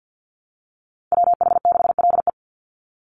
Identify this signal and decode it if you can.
Morse code, heard as a keyed tone, about 50 wpm, 700 Hz. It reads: WH6LE
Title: NA SOTA Activator Callsigns 50wpm